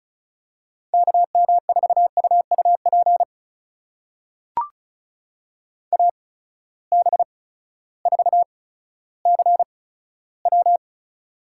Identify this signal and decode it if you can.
Morse code, heard as a keyed tone, about 34 wpm, spaced as for 16 wpm, 700 Hz. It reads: KM4UUP AB4CW
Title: USA Call Signs - 8x Character Spacing - 35wpm